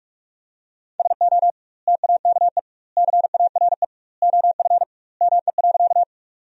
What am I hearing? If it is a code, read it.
Morse code, heard as a keyed tone, about 41 wpm, 700 Hz. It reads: SO TAKE CARE OF ME.
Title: 2x Word-Spacing - Sentences from Top 300 Words - 45wpm